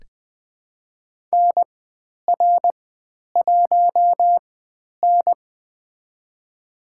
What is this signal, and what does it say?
Morse code, heard as a keyed tone, about 20 wpm, spaced as for 10 wpm, 700 Hz. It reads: NR1N
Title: 102 Lesson 17g - Letter u - Call Signs 20wpm